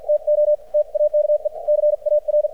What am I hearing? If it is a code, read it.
Morse code, heard as a keyed tone, about 50 wpm, spaced as for 43 wpm, 600 Hz. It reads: TO TAKE CAR
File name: archived / CRSnet